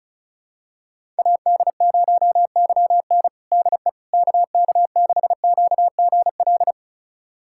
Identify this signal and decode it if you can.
Morse code, heard as a keyed tone, about 36 wpm, 700 Hz. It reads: AD0YN DE KK6QGL
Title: USA Callsign DE Callsign 35wpm